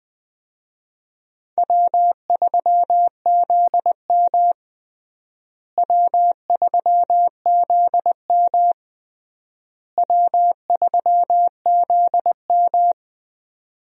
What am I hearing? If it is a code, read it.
Morse code, heard as a keyed tone, about 21 wpm, 700 Hz. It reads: W3ZM W3ZM W3ZM
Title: USA Call Signs - 4 Characters - Repeated 3x - 20wpm